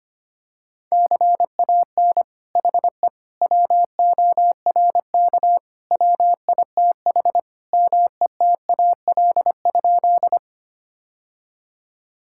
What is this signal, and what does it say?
Morse code, heard as a keyed tone, about 25 wpm, 700 Hz. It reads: CAN HE WORK WITH METAL?
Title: Sentences from Top 1000 Words - Review Part 6 - 25wpm